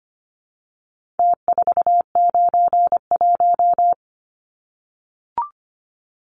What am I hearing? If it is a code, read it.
Morse code, heard as a keyed tone, about 25 wpm, 700 Hz. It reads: T491
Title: Numbers - Contest Style - 4-Digit 25wpm